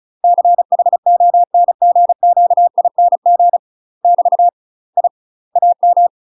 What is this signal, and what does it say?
Morse code, heard as a keyed tone, about 34 wpm, 700 Hz. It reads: CHONGQING <BT> I AM
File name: UK MASTER QSOs TRAINING / 35WPM / UK MASTER QSOs 0015